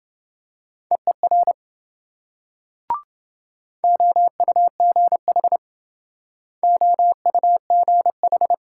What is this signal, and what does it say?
Morse code, heard as a keyed tone, about 27 wpm, 700 Hz. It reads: EER OUGH OUGH